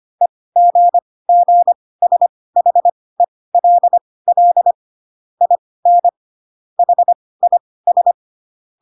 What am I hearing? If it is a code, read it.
Morse code, heard as a keyed tone, about 25 wpm, spaced as for 18 wpm, 700 Hz. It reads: EGGSHELL IN HIS